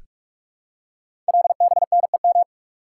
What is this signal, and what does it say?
Morse code, heard as a keyed tone, about 44 wpm, 700 Hz. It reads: LBNEM
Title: Code Groups - 5 Letters - 45wpm